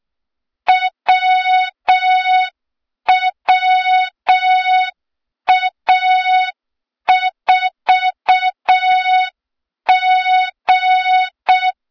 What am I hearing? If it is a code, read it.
Morse code, heard as a keyed tone, about 6 wpm, 3700 Hz, with noise. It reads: WWA4G
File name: Morze-HA2MN / hang / 23 GYAKORLAT